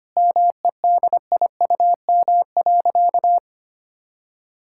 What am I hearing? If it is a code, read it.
Morse code, heard as a keyed tone, about 25 wpm, 700 Hz. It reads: MEDIUM.
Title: Sentences from Top 1000 Words - Review Part 6 - 25wpm